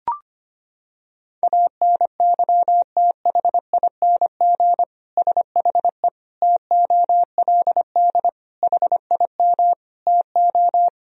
Morse code, heard as a keyed tone, about 25 wpm, 700 Hz. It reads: ANYTHING SHE TOLD HIM TO
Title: Sentences from Top 1000 Words - Review Part 3 - 25wpm